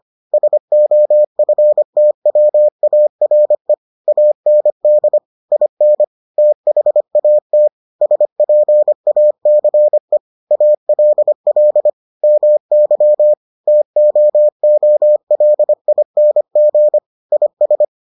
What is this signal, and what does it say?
Morse code, heard as a keyed tone, about 25 wpm, 600 Hz. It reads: SOFTWARE AND IN THAT SPACE ALL MY TOOLING IS